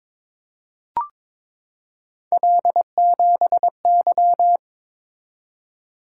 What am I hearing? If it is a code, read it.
Morse code, heard as a keyed tone, about 22 wpm, 700 Hz. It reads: L7Y
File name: practice / 22wpm / Three Letter-Number Combinations - 22wpm